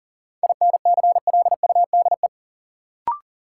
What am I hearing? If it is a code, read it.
Morse code, heard as a keyed tone, about 40 wpm, 700 Hz. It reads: INCLUDE
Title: Sentences from Top 1000 Words - Review Part 3 - New Word Drill - 40wpm